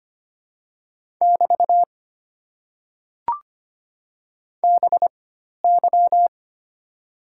Morse code, heard as a keyed tone, about 25 wpm, spaced as for 12 wpm, 700 Hz. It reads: <BT> BY
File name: Top 90 QSO Elements - 4x Character Spacing - 25wpm